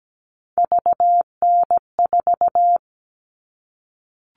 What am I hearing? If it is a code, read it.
Morse code, heard as a keyed tone, about 17 wpm, 700 Hz. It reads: VN4
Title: All Cut Numbers - 3-Digit 17wpm